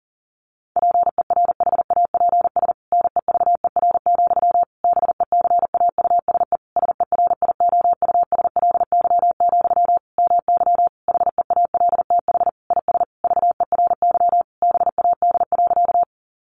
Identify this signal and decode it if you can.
Morse code, heard as a keyed tone, about 40 wpm, 700 Hz. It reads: PERHAPS NEVER, BECAUSE SERIOUSLY, MY HEALTH IS VERY BAD.
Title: Sentences from Top 1200 Words 40wpm